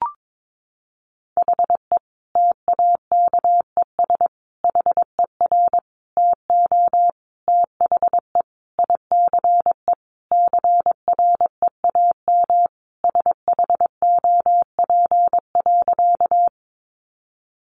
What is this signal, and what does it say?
Morse code, heard as a keyed tone, about 22 wpm, 700 Hz. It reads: HE TAKES HER TO THE ICE CREAM SHOP.
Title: Easy Phrases 22wpm